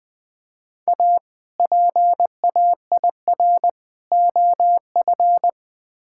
Morse code, heard as a keyed tone, about 21 wpm, 700 Hz. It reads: A PAIR OF
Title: Build A Phrase - 3-Word Phrases - 20wpm